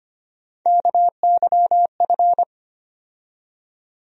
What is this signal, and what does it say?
Morse code, heard as a keyed tone, about 25 wpm, 700 Hz. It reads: KYF
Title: USA Call Sign Postfixes - 3 Letters 25wpm